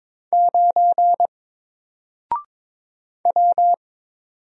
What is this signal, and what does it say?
Morse code, heard as a keyed tone, about 22 wpm, 700 Hz. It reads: 9 W
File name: Single-Letter-Number-22wpm